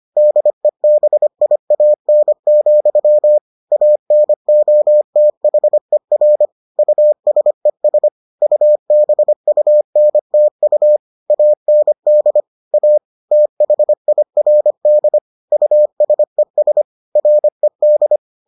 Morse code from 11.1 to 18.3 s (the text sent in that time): AND A THIRD USES RED